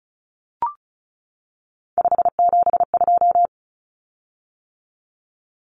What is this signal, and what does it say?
Morse code, heard as a keyed tone, about 34 wpm, 700 Hz. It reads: 572
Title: Common Cut Numbers - 3-Digit 35wpm